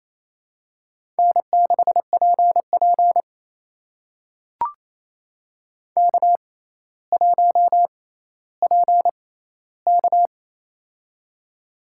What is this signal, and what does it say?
Morse code, heard as a keyed tone, about 28 wpm, spaced as for 14 wpm, 700 Hz. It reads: N6PP K1PK
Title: USA Call Signs - 4 Characters - 6x Character Spacing - 28wpm